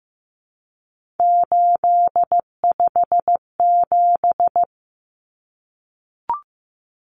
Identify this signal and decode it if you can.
Morse code, heard as a keyed tone, about 15 wpm, 700 Hz. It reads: 857
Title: Numbers - 3-Digit 15wpm